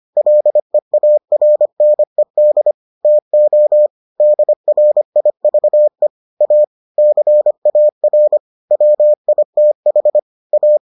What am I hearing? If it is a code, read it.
Morse code, heard as a keyed tone, about 25 wpm, 600 Hz. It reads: LEARNED TO DRIVE A CAR WITH A